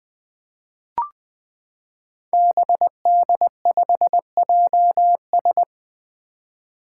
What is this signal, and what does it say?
Morse code, heard as a keyed tone, about 20 wpm, 700 Hz. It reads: BD5JS
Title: Five Challenging Letter-Number Combinations 20wpm